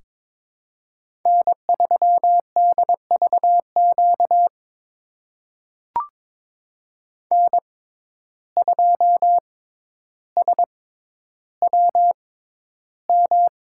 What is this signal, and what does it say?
Morse code, heard as a keyed tone, about 23 wpm, spaced as for 11 wpm, 700 Hz. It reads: N3DVQ N2SWM
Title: USA Call Signs - 5 Characters - 6x Character Spacing - 22wpm